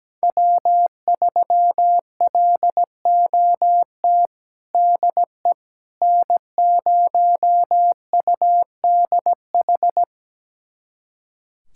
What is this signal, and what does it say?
Morse code, heard as a keyed tone, about 17 wpm, 700 Hz. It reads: W3LOT DE N0UDH
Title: USA Callsign DE Callsign - 5 Characters - 17wpm